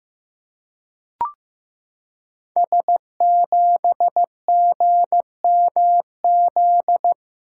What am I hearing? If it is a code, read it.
Morse code, heard as a keyed tone, about 15 wpm, 700 Hz. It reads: S7GMZ
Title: Code Groups - 5 Letter:Numbers - 15wpm